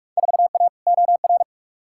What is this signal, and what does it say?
Morse code, heard as a keyed tone, about 46 wpm, 700 Hz. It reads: 4A OR
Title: Field Day Exchanges - 45wpm